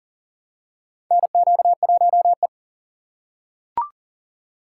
Morse code, heard as a keyed tone, about 40 wpm, 700 Hz. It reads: NQ1E
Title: USA Call Signs - 4 Characters - 3x Character Spacing - 40wpm